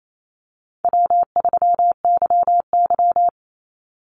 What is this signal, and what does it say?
Morse code, heard as a keyed tone, about 28 wpm, 700 Hz. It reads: W3YY
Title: Call Signs Repeated 2x - 28wpm